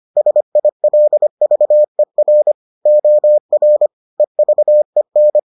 Morse code, heard as a keyed tone, about 25 wpm, 600 Hz. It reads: SILVER OR EVEN